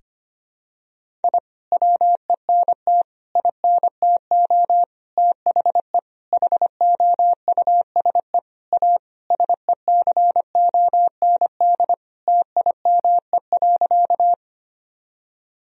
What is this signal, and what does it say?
Morse code, heard as a keyed tone, about 25 wpm, 700 Hz. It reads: I WENT INTO THE HOUSE A SECOND TIME.
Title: Sentences from Top 1000 Words - Review Part 1 - 25wpm